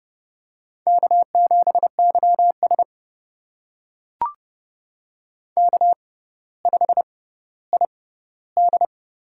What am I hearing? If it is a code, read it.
Morse code, heard as a keyed tone, about 30 wpm, spaced as for 15 wpm, 700 Hz. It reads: K7YS K5ID
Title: USA Call Signs - 4 Characters - 6x Character Spacing - 30wpm